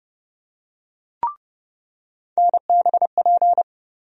Morse code, heard as a keyed tone, about 30 wpm, 700 Hz. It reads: NBP
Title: USA Call Sign Postfixes - 3 Letters 30wpm